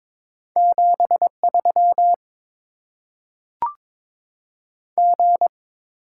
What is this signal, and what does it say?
Morse code, heard as a keyed tone, about 22 wpm, 700 Hz. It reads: 7 3 G
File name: Top 20 QSO Elements - 8x Character Spacing - 22wpm